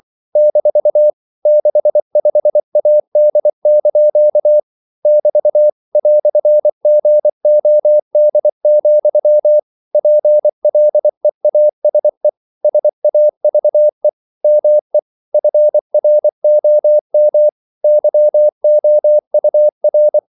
Morse code, heard as a keyed tone, about 24 wpm, 600 Hz. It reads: - 65AD) <BT> "GOD, PLEASE SAVE ME FROM YOUR
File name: CW / Practice / EnglishQuotations0224